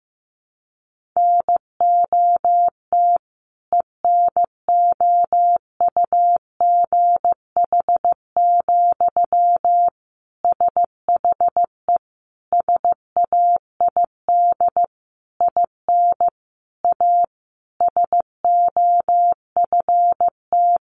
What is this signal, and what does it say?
Morse code, heard as a keyed tone, about 15 wpm, 700 Hz. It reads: NOT ENOUGH, SHE SAID IN A SOFT
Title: Sentences from Top 1500 Words 15wpm